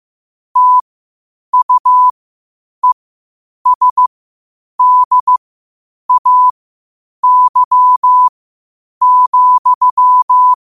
Morse code, heard as a keyed tone, about 15 wpm, spaced as for 8 wpm, 1000 Hz. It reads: TUESDAY,
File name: qotdfiles / 8wpm